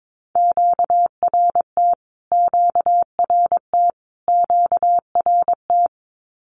Morse code, heard as a keyed tone, about 22 wpm, 700 Hz. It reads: QRT QRT QRT
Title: Q-Codes Repeated 3x 22wpm